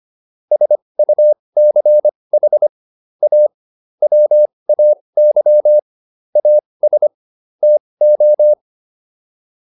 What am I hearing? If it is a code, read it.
Morse code, heard as a keyed tone, about 25 wpm, spaced as for 20 wpm, 600 Hz. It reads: SUCH A WAY AS TO